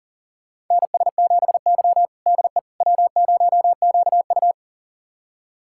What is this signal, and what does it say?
Morse code, heard as a keyed tone, about 40 wpm, 700 Hz. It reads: NS7Y DE W0QU
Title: USA Callsign DE Callsign - 4 Characters - 40wpm